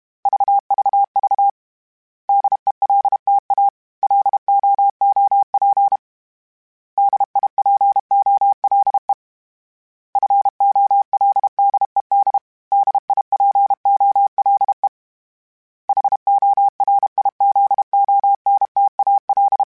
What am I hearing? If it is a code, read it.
Morse code, heard as a keyed tone, about 32 wpm, 800 Hz. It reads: VVV DELTA LOOP DIPOLE FOLDED DIPOLE HORIZONTAL
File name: fromSMs / FromSMs mp3s / Antennas@30